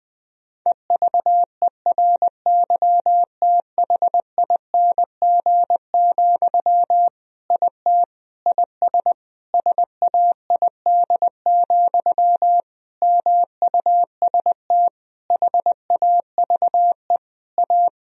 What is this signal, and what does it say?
Morse code, heard as a keyed tone, about 21 wpm, 700 Hz. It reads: EVERYTHING, IT IS SAID, MUST HAVE A
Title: Sentences from Top 1000 Words - Review Part 1 - 20wpm